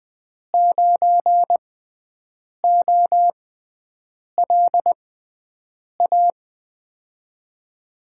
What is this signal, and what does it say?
Morse code, heard as a keyed tone, about 20 wpm, spaced as for 7 wpm, 700 Hz. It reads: 9OLA
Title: USA Call Signs - 6x Character Spacing - 20wpm